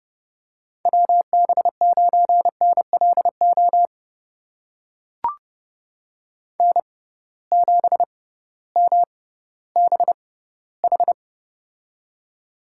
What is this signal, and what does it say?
Morse code, heard as a keyed tone, about 31 wpm, spaced as for 16 wpm, 700 Hz. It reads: WB9NLO N7MBH
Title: USA Call Signs - 6x Character Spacing - 30wpm